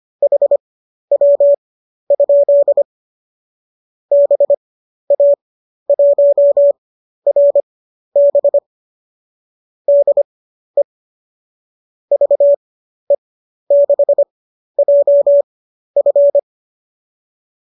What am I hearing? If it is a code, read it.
Morse code, heard as a keyed tone, about 25 wpm, spaced as for 12 wpm, 570 Hz. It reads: HW? BA1RB DE VE6JF